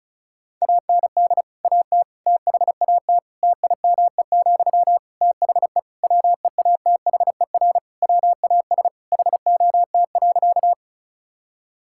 Morse code, heard as a keyed tone, about 36 wpm, 700 Hz. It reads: AND AT THAT TIME, THE WEATHER WAS HOT.
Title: Sentences from Top 1000 Words - Review Part 5 - 35wpm